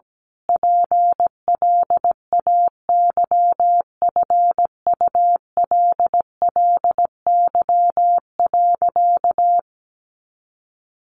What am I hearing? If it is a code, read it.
Morse code, heard as a keyed tone, about 17 wpm, 700 Hz. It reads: PLAYFULLY.
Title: A Sound Of Thunder 17wpm